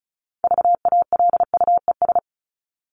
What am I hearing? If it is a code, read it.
Morse code, heard as a keyed tone, about 34 wpm, 700 Hz. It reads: VALUES
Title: ES:S Suffix - Top 100 Words with Base Words 35wpm